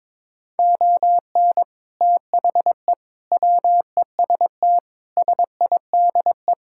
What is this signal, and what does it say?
Morse code, heard as a keyed tone, about 22 wpm, 700 Hz. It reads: ON THE WEST SIDE
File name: practice / 22wpm / Sets of 6 Words - 22wpm